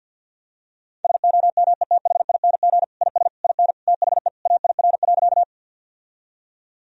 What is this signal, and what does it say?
Morse code, heard as a keyed tone, about 50 wpm, 700 Hz. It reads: SOMETHING IS IN THE AIR.
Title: Sentences from Top 1000 Words - Review Part 1 - 50wpm